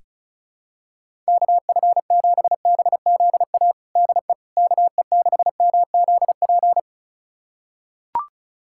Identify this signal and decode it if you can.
Morse code, heard as a keyed tone, about 36 wpm, 700 Hz. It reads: KF7BZA DE KE6MZP
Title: USA Callsign DE Callsign - 6 Characters - 35wpm